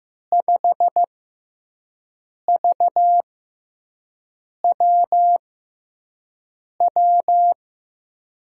Morse code, heard as a keyed tone, about 15 wpm, spaced as for 5 wpm, 700 Hz. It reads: 5VWW
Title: USA Call Signs - 6 Characters - 6x Character Spacing - 15wpm